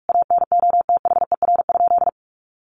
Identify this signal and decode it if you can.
Morse code, heard as a keyed tone, about 44 wpm, 700 Hz. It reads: ANOTHER?
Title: Sentences from Top 600 Words 45wpm